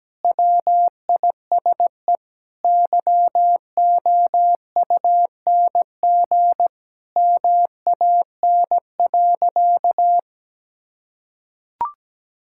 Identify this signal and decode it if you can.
Morse code, heard as a keyed tone, about 17 wpm, 700 Hz. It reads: WISE YOUNG MAN.
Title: Sentences from Top 1600 Words 17wpm